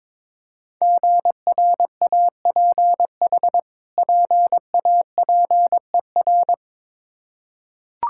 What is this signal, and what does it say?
Morse code, heard as a keyed tone, about 22 wpm, 700 Hz. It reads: GRAPH PAPER
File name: Difficult Words with Hints - 22wpm